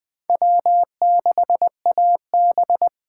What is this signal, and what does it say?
Morse code, heard as a keyed tone, about 20 wpm, 700 Hz. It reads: W6AB
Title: USA Callsign DE Callsign - 4 Characters - 20wpm